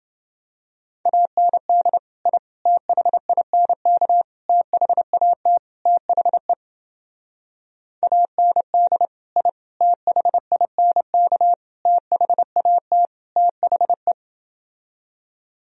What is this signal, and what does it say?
Morse code, heard as a keyed tone, about 28 wpm, 700 Hz. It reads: AND I THINK THAT THE AND I THINK THAT THE